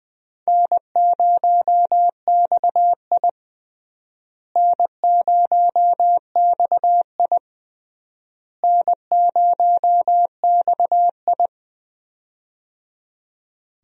Morse code, heard as a keyed tone, about 20 wpm, 700 Hz. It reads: N0XI N0XI N0XI
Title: USA Call Signs - 4 Characters - Repeated 3x - 20wpm